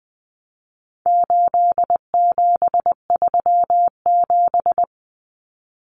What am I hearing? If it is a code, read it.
Morse code, heard as a keyed tone, about 20 wpm, 700 Hz. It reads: 8737
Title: Common Cut Numbers - 4-Digit 20wpm